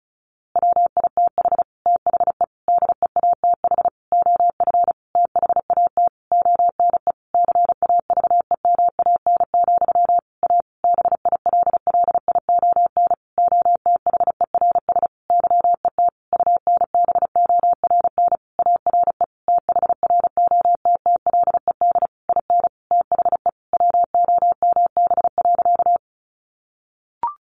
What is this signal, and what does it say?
Morse code, heard as a keyed tone, about 34 wpm, 700 Hz. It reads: WITH THE DEATH OF THAT ONE CAVEMAN, A BILLION OTHERS YET UNBORN ARE THROTTLED IN THE WOMB.
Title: A Sound Of Thunder 35wpm